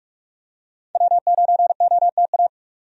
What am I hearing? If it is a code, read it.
Morse code, heard as a keyed tone, about 46 wpm, 700 Hz. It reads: W9OTA
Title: USA Call Signs - 5 Characters - 45wpm